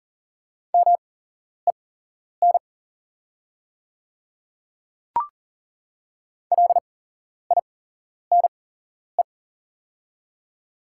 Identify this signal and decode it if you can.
Morse code, heard as a keyed tone, about 39 wpm, spaced as for 11 wpm, 700 Hz. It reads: MEN LINE
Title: 8x Character Spacing - Top 200 Words - 40wpm